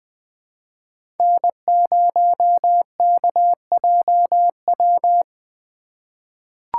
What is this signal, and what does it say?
Morse code, heard as a keyed tone, about 21 wpm, 700 Hz. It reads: N0KJW
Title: USA Call Signs - 4x Character Spacing - 20wpm